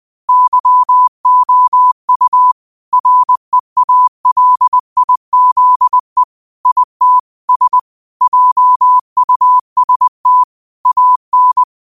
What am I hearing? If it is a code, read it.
Morse code, heard as a keyed tone, about 20 wpm, 1000 Hz. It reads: YOU REALIZE IT S JUST AN